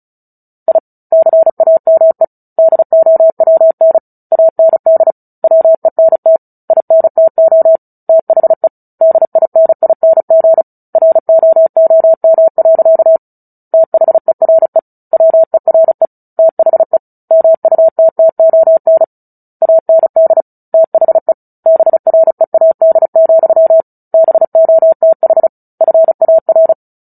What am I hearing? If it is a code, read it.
Morse code, heard as a keyed tone, about 35 wpm, 650 Hz. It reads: I CAME DOWN AND WENT INTO THE DINING ROOM. THERE WERE THE MUTTON AND THE BREAD, BOTH FAR
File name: Book Two / 09-Chapter 9 - 35 WPM